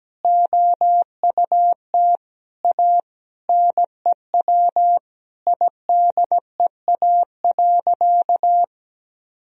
Morse code, heard as a keyed tone, about 17 wpm, 700 Hz. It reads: OUT A NEW IDEA.
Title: Sentences from Top 1000 Words - Review Part 1 - 17wpm